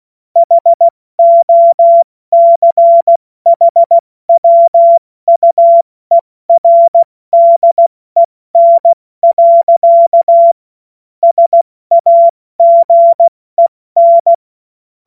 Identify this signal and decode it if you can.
Morse code, heard as a keyed tone, about 16 wpm, 680 Hz. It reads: HOCHWUERDEN. SAGEN
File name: Morsekurs / Witze / Witz 09. Dezember 2025 70 F80